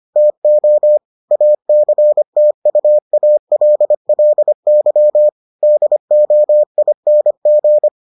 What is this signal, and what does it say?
Morse code, heard as a keyed tone, about 25 wpm, 600 Hz. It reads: TO ACTUALLY DOING